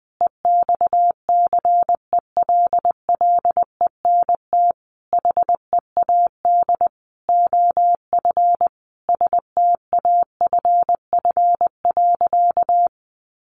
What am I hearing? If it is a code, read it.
Morse code, heard as a keyed tone, about 20 wpm, 700 Hz. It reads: EXCELLENT HEAD OF STAFF.
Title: Sentences from Top 1200 Words 20wpm